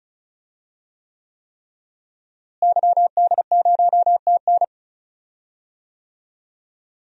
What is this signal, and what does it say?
Morse code, heard as a keyed tone, about 36 wpm, 700 Hz. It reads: YD0TN
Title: DXCC-100 Call Signs - 5 Characters - Repeated 3x - 35wpm